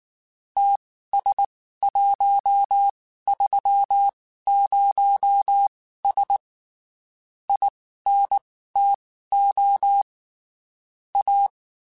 Morse code, heard as a keyed tone, about 19 wpm, spaced as for 14 wpm, 800 Hz. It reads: TS130S INTO A
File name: (spaced) / QSOs Spaced-mp3s / WD0BGP 1 @18 sp